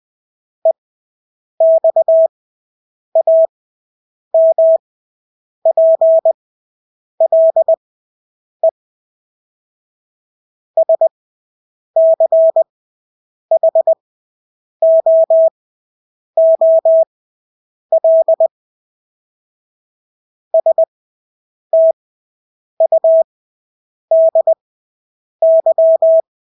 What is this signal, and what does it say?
Morse code, heard as a keyed tone, about 20 wpm, spaced as for 8 wpm, 650 Hz. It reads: EXAMPLE SCHOOL STUDY
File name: Engelske ord 20/8 wpm